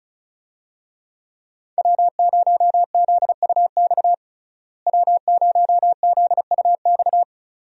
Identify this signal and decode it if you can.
Morse code, heard as a keyed tone, about 34 wpm, 700 Hz. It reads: W0ZUX W0ZUX
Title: USA Call Signs - 5 Characters - Repeated 2x - 35wpm